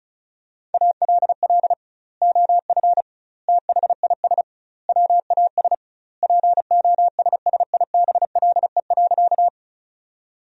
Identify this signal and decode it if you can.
Morse code, heard as a keyed tone, about 36 wpm, 700 Hz. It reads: ALL OF THIS WAS POSSIBLE.
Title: 2x Word-Spacing - Sentences from Top 500 Words - 35wpm